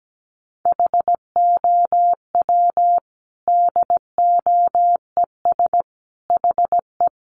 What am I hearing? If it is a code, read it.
Morse code, heard as a keyed tone, about 17 wpm, 700 Hz. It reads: HOW DOES HE
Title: Sentences from Top 200 Words 17wpm